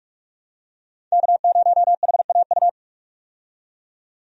Morse code, heard as a keyed tone, about 47 wpm, 700 Hz. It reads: K0HAU
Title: USA Call Signs - 5 Characters - 45wpm